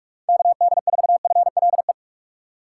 Morse code, heard as a keyed tone, about 46 wpm, 700 Hz. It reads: KD4FLE
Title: USA Call Signs - 6 Characters - 4x Character Spacing - 45wpm